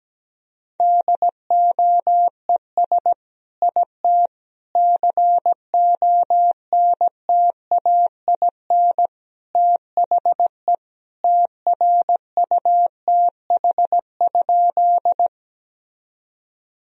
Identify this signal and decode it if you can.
Morse code, heard as a keyed tone, about 17 wpm, 700 Hz. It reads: DOES IT CONTAIN THE TRUTH?
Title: Sentences from Top 1000 Words - Review Part 2 - 17wpm